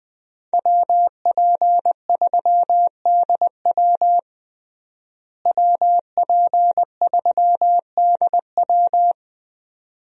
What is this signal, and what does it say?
Morse code, heard as a keyed tone, about 20 wpm, 700 Hz. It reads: WP3DW WP3DW
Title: USA Call Signs - 5 Characters - Repeated 2x - 20wpm